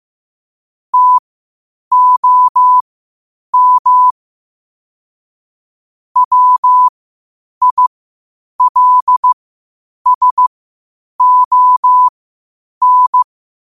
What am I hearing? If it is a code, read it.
Morse code, heard as a keyed tone, about 15 wpm, spaced as for 8 wpm, 1000 Hz. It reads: TOM WILSON
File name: qotdfiles / 8wpm